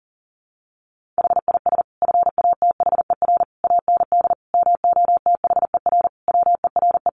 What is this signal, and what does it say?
Morse code, heard as a keyed tone, about 40 wpm, 700 Hz. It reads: HIS FATHER AND MOTHER WERE
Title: Sentences from Top 1500 Words 40wpm